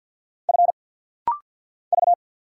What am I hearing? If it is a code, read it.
Morse code, heard as a keyed tone, about 50 wpm, 700 Hz. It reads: F V
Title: Single Letters - Rapid-Fire 50wpm